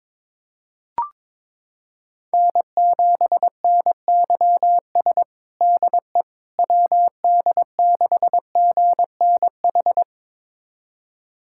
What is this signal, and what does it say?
Morse code, heard as a keyed tone, about 22 wpm, 700 Hz. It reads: N7NYS DE WD6GNH
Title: USA Callsign DE Callsign 22wpm